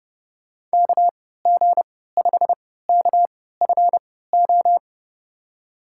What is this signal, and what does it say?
Morse code, heard as a keyed tone, about 29 wpm, spaced as for 17 wpm, 700 Hz. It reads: KG5KFO
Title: USA Call Signs - 6 Characters - 3x Character Spacing - 30wpm